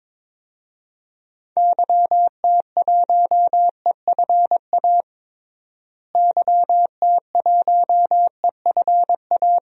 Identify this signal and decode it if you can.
Morse code, heard as a keyed tone, about 22 wpm, 700 Hz. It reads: YT1EFA YT1EFA
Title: DXCC-100 Call Signs - 6 Characters - Repeated 2x - 22wpm